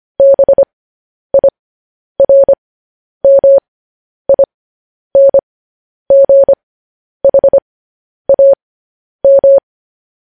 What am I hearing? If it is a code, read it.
Morse code, heard as a keyed tone, about 25 wpm, spaced as for 10 wpm, 550 Hz. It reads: BIRMINGHAM
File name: podcast / News headlines-1x-25x10